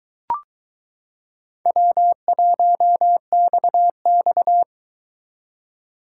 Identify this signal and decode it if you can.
Morse code, heard as a keyed tone, about 22 wpm, 700 Hz. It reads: W1XX